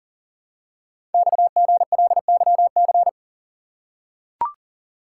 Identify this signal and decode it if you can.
Morse code, heard as a keyed tone, about 40 wpm, 700 Hz. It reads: XGLYC
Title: Code Groups - 5 Letters - 40wpm